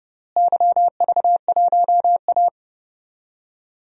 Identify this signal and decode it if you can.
Morse code, heard as a keyed tone, about 30 wpm, 700 Hz. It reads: YV1A
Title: DXCC-100 Call Signs - 4 Characters - Repeated 2x - 30wpm